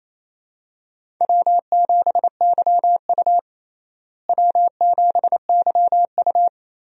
Morse code, heard as a keyed tone, about 28 wpm, 700 Hz. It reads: W7YU W7YU
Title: USA Call Signs - 4 Characters - Repeated 3x - 28wpm